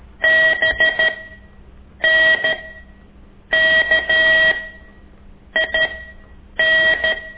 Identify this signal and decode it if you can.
Morse code, heard as a keyed tone, about 13 wpm, spaced as for 7 wpm, 1900 Hz. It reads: BNKIN
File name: ik1weg2 / CORSI CW